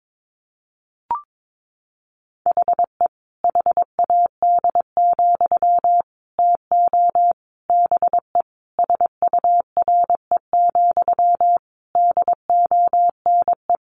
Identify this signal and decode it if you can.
Morse code, heard as a keyed tone, about 22 wpm, 700 Hz. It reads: HE HAD, TO BE SURE, DONE
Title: Sentences from Top 500 Words 22wpm